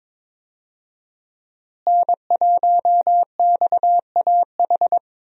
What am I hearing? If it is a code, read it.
Morse code, heard as a keyed tone, about 22 wpm, 700 Hz. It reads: N1XAH
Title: USA Call Signs - 5 Characters - Repeated 2x - 22wpm